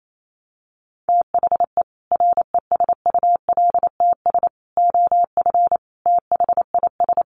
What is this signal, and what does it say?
Morse code, heard as a keyed tone, about 28 wpm, 700 Hz. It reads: THE RESULTS OF THIS
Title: Sets of 5 Words 28wpm